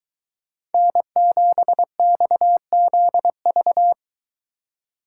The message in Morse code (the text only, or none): N7XZV